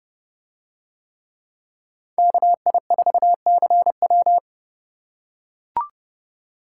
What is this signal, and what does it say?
Morse code, heard as a keyed tone, about 30 wpm, 700 Hz. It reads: KI4CW